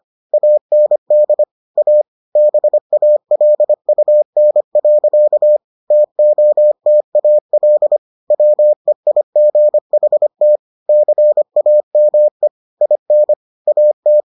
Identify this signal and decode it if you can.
Morse code, heard as a keyed tone, about 25 wpm, 600 Hz. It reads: AND A BALUN. TOTAL WEIGHT CAME IN AT